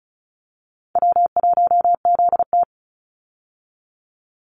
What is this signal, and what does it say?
Morse code, heard as a keyed tone, about 34 wpm, 700 Hz. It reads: W1ZT
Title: Easy Call Signs Repeated 2x - 35wpm